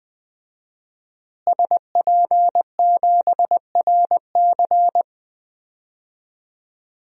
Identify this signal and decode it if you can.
Morse code, heard as a keyed tone, about 21 wpm, 700 Hz. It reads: SP7RC
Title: DXCC-100 Call Signs - 5 Characters - Repeated 2x - 20wpm